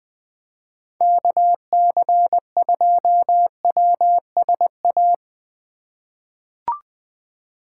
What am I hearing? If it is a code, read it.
Morse code, heard as a keyed tone, about 20 wpm, 700 Hz. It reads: KC2WSA
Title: USA Call Signs - 3x Character Spacing - 20wpm